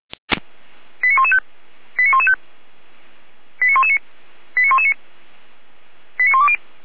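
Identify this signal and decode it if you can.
Five-tone selective call (ZVEI), heard as quick runs of five tones, tones of about 70 ms. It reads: 89115 89115 89119 89119 89120
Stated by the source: ZVEI five-tone selective call received on SDR.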